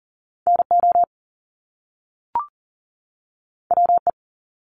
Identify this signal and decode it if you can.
Morse code, heard as a keyed tone, about 40 wpm, 700 Hz. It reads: NO WE
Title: Top Two Letter Combinations in Words 40wpm